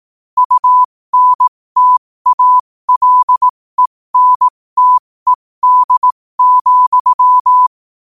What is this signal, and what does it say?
Morse code, heard as a keyed tone, about 18 wpm, spaced as for 15 wpm, 1000 Hz. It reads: UNTALENTED,
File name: qotdfiles / 15wpm